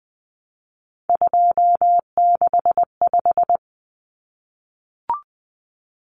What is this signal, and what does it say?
Morse code, heard as a keyed tone, about 20 wpm, 700 Hz. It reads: 265
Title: Numbers - 3-Digit 20wpm